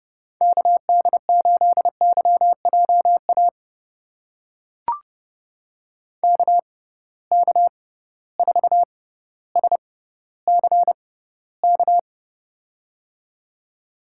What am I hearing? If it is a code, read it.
Morse code, heard as a keyed tone, about 32 wpm, spaced as for 16 wpm, 700 Hz. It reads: KD8YJA KK4SCK
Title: USA Call Signs - 6 Characters - 6x Character Spacing - 30wpm